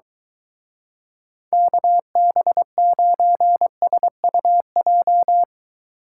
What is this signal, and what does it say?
Morse code, heard as a keyed tone, about 23 wpm, 700 Hz. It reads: KB9SUJ